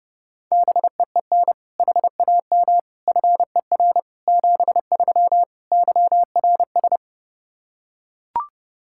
Morse code, heard as a keyed tone, about 30 wpm, 700 Hz. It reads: BEEN HAM FER 73 YRS
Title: Practice QSO - Exchange 3 - More Info - 30wpm